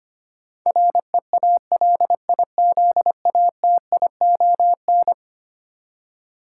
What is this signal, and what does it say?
Morse code, heard as a keyed tone, about 25 wpm, 700 Hz. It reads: REALIZATION
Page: morse-code-ninja-practice-25wpm